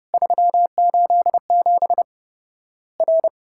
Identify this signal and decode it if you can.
Morse code, heard as a keyed tone, about 29 wpm, 700 Hz. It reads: 387 R
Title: Worked All Europe DX Contest - QTC Exchange - 30wpm